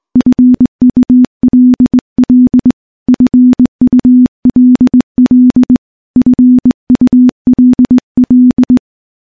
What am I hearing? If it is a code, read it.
Morse code, heard as a keyed tone, about 20 wpm, 250 Hz. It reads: FULL FULL FULL
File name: morse / morse-training-20WPM